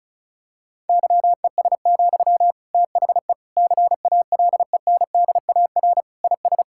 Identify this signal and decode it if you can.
Morse code, heard as a keyed tone, about 36 wpm, 700 Hz. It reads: YES, THE CALENDAR IS
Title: Sentences from Top 1600 Words 35wpm